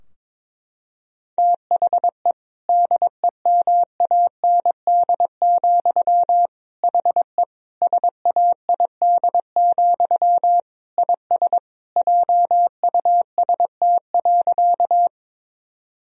Morse code, heard as a keyed tone, about 22 wpm, 700 Hz. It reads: THE DEMAND, HE SAID, IS JUST.
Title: Sentences from Top 1000 Words - Review Part 4 - 22wpm